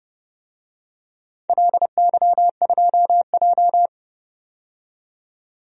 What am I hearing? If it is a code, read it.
Morse code, heard as a keyed tone, about 30 wpm, 700 Hz. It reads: LY2J
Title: DXCC-100 Call Signs - 4 Characters - 30wpm